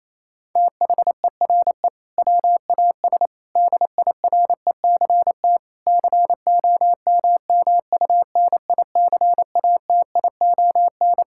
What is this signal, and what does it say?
Morse code, heard as a keyed tone, about 27 wpm, 700 Hz. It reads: THERE WAS DIRECT COMMUNICATION
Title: Sentences from Top 1000 Words - Review Part 6 - 28wpm